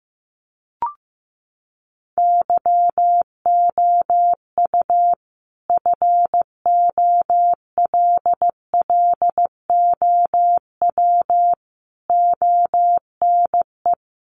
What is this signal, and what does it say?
Morse code, heard as a keyed tone, about 14 wpm, 700 Hz. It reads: YOU FOLLOW ONE
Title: Sentences from Top 1300 Words 15wpm